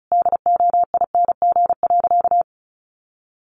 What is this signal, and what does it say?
Morse code, heard as a keyed tone, about 34 wpm, 700 Hz. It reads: DOING.
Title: Sentences from Top 1400 Words 35wpm